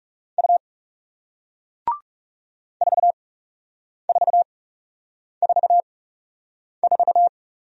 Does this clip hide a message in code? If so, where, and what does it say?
Yes: Morse code from 0.4 to 7.3 s, heard as a keyed tone, about 40 wpm, spaced as for 8 wpm, 700 Hz. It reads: U 4444